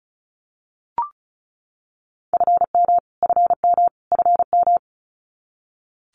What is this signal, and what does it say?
Morse code, heard as a keyed tone, about 34 wpm, 700 Hz. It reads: FM FM FM
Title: Abbreviations Repeated 3x - 35wpm